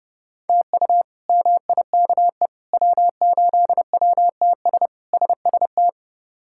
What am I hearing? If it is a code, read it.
Morse code, heard as a keyed tone, about 30 wpm, 700 Hz. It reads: TU MIKE W8WTS SST
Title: K1USN SST QSO Practice - 30wpm